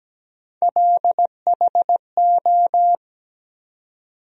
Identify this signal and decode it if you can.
Morse code, heard as a keyed tone, about 17 wpm, 700 Hz. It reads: LHO
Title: USA Call Sign Postfixes - 3 Letters 17wpm